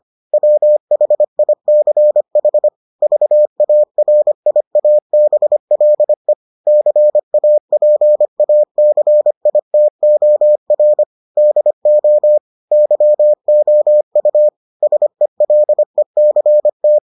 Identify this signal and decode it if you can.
Morse code, heard as a keyed tone, about 25 wpm, 600 Hz. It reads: WHICH VARIABLE CAPACITOR DO YOU SELECT